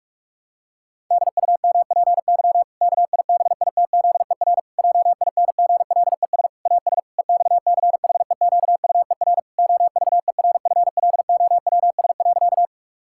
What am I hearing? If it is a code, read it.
Morse code, heard as a keyed tone, about 41 wpm, 700 Hz. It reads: DUMPY KIBITZER JINGLES AS EXCHEQUER OVERFLOWS.
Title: Pangram Challenge v2 - 45wpm